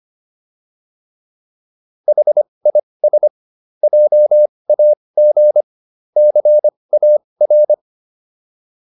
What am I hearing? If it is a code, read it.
Morse code, heard as a keyed tone, about 25 wpm, spaced as for 20 wpm, 600 Hz. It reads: HIS JAG CAR